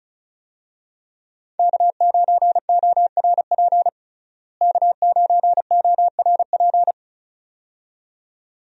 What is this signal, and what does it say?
Morse code, heard as a keyed tone, about 34 wpm, 700 Hz. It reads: K9ORP K9ORP
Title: USA Call Signs - 5 Characters - Repeated 2x - 35wpm